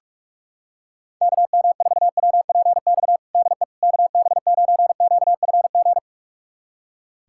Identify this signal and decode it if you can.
Morse code, heard as a keyed tone, about 42 wpm, 700 Hz. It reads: KM4WPX DE KB9QFG
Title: USA Callsign DE Callsign - 6 Characters - 45wpm